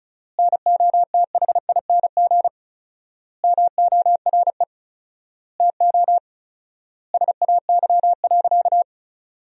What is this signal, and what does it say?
Morse code, heard as a keyed tone, about 34 wpm, 700 Hz. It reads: NOTHING MORE TO SAY.
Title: 4x Word-Spacing - Sentences from Top 500 Words - 35wpm